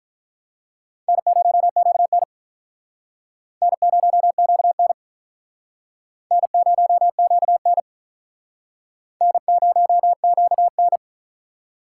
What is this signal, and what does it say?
Morse code, heard as a keyed tone, about 43 wpm, 700 Hz. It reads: N0QN N0QN N0QN N0QN
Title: Speed Racing - USA 1x2 Call Signs 35wpm